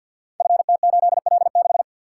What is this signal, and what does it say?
Morse code, heard as a keyed tone, about 52 wpm, 700 Hz. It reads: FT8L6
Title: Code Groups - 5 Letter:Numbers - 50wpm